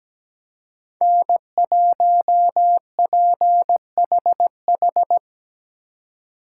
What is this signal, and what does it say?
Morse code, heard as a keyed tone, about 20 wpm, 700 Hz. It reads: N1PHH
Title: Speed Racing - USA Call Signs 17wpm